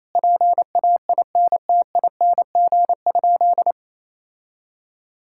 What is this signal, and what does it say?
Morse code, heard as a keyed tone, about 28 wpm, 700 Hz. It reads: PAINTING?
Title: Sentences from Top 1000 Words - Review Part 6 - 28wpm